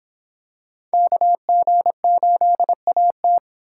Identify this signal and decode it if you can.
Morse code, heard as a keyed tone, about 25 wpm, 700 Hz. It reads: KG8AT